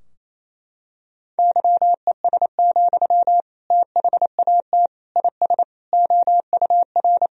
Morse code, heard as a keyed tone, about 27 wpm, 700 Hz. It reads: YES, THAT IS OUR
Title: Sentences from Top 1000 Words - Review Part 3 - 28wpm